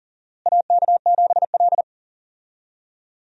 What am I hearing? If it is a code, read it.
Morse code, heard as a keyed tone, about 40 wpm, 700 Hz. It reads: AK7L
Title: USA Call Signs - 4 Characters - Repeated 3x - 40wpm